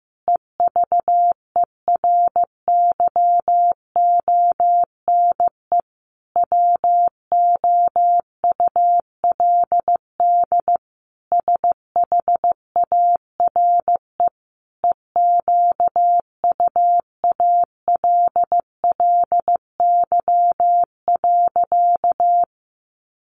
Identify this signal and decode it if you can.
Morse code, heard as a keyed tone, about 15 wpm, 700 Hz. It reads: EVERYONE WOULD SHARE EQUALLY.
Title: Sentences from Top 1400 Words 15wpm